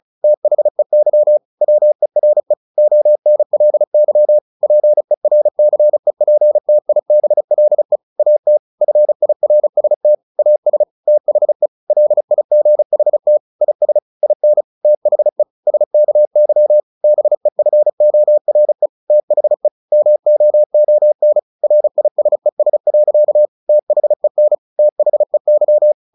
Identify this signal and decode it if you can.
Morse code, heard as a keyed tone, about 34 wpm, 600 Hz. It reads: THEY WERE ONLY PERCEPTIBLE AT FIRST AS THE LIGHT IS IN THE SKY BEFORE THE MOON RISES. THEN THEY
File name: the old man and the sea 2 35wpm